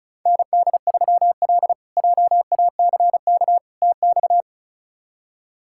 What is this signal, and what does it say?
Morse code, heard as a keyed tone, about 36 wpm, 700 Hz. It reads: ND3L JACK TX
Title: K1USN SST QSO Practice - 35wpm